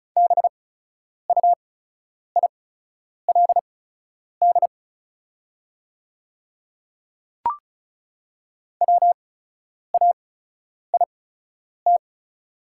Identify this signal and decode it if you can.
Morse code, heard as a keyed tone, about 36 wpm, spaced as for 10 wpm, 700 Hz. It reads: BUILD WAIT
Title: 8x Character Spacing - Top 500 Words - 35wpm